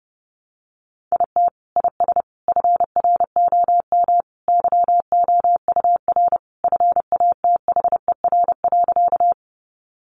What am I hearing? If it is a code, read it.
Morse code, heard as a keyed tone, about 28 wpm, 700 Hz. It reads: IT IS FROM YOUR FATHER.
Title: Sentences from Top 300 Words 30wpm